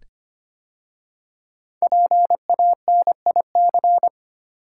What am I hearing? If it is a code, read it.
Morse code, heard as a keyed tone, about 25 wpm, 700 Hz. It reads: PANIC
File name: 5 Letter Words - First 3 Letters Given - 25wpm